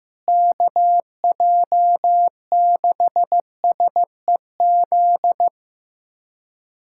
Morse code, heard as a keyed tone, about 15 wpm, 700 Hz. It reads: KJ6SEZ
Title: USA Callsign DE Callsign - 6 Characters - 15wpm